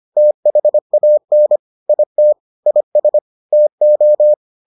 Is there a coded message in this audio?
Yes: Morse code from 0.2 to 4.3 s, heard as a keyed tone, about 25 wpm, 600 Hz. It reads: THAN IT IS TO